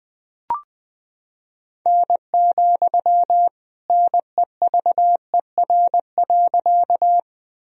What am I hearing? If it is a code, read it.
Morse code, heard as a keyed tone, about 20 wpm, 700 Hz. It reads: N, NEVER.
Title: Sentences from Top 300 Words and Abbreviations 20wpm